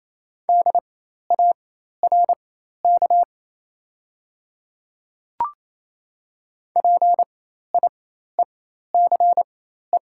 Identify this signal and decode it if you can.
Morse code, heard as a keyed tone, about 27 wpm, spaced as for 13 wpm, 700 Hz. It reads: DARK PIECE
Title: 4x Character Spacing - Top 500 Words - 28wpm